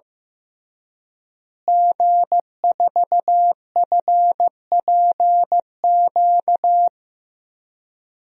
Morse code, heard as a keyed tone, about 15 wpm, 700 Hz. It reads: G4FPQ
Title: DXCC-100 Call Signs - 5 Characters - Repeated 3x - 15wpm